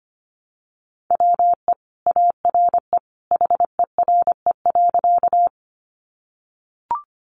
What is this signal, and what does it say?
Morse code, heard as a keyed tone, about 25 wpm, 700 Hz. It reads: WE ARE HERE.
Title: Sentences from Top 200 Words 25wpm